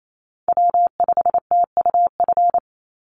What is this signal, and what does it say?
Morse code, heard as a keyed tone, about 28 wpm, 700 Hz. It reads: W5TUF
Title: Call Signs Repeated 3x - 28wpm